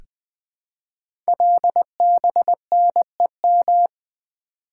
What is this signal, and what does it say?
Morse code, heard as a keyed tone, about 20 wpm, 700 Hz. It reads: LBNEM
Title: Code Groups - 5 Letters - 20wpm